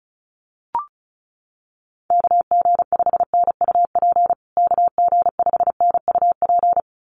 Morse code, heard as a keyed tone, about 34 wpm, 700 Hz. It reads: KG5NUP KG5NUP
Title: USA Call Signs Repeated 2x 35wpm